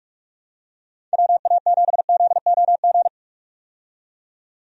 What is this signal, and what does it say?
Morse code, heard as a keyed tone, about 45 wpm, 700 Hz. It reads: WA7ZOG
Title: USA Call Signs - 6 Characters - 45wpm